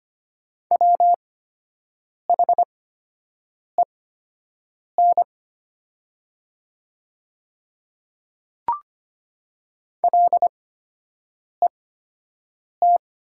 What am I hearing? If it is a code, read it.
Morse code, heard as a keyed tone, about 25 wpm, spaced as for 7 wpm, 700 Hz. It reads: WHEN LET
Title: 8x Character Spacing - Top 500 Words - 25wpm